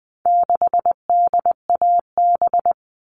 Morse code, heard as a keyed tone, about 20 wpm, 700 Hz. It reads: 6DAB
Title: All Cut Numbers - 4-Digit 20wpm